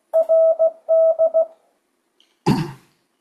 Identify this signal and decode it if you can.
Morse code, heard as a keyed tone, about 15 wpm, 650 Hz. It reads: RD
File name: 2025cw / speedup / 16wpm